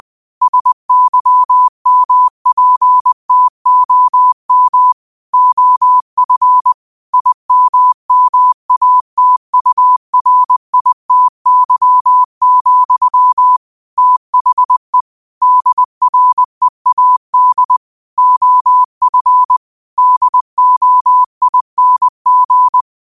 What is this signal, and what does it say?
Morse code, heard as a keyed tone, about 20 wpm, 1000 Hz. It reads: SYMPTOM OF IMMATURITY, THE DREAD OF DOING
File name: qotdfiles / 20wpm